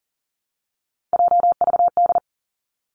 Morse code, heard as a keyed tone, about 38 wpm, 700 Hz. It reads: JVD
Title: Three Challenging Letter-Number Combinations 40wpm